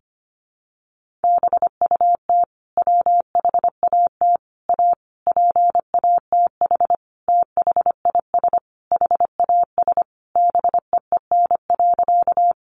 Morse code, heard as a keyed tone, about 25 wpm, 700 Hz. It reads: BUT WHAT A PATH THIS HAS BEEN.
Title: Sentences from Top 1400 Words 25wpm